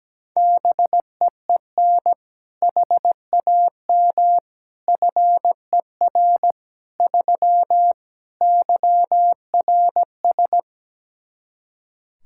Morse code, heard as a keyed tone, about 17 wpm, 700 Hz. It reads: BEEN HAM FER 3 YRS
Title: K1USN SST QSO Practice 17wpm